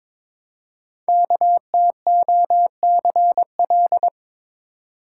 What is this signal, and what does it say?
Morse code, heard as a keyed tone, about 23 wpm, 700 Hz. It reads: KTOCL
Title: Code Groups - 5 Letters - 22wpm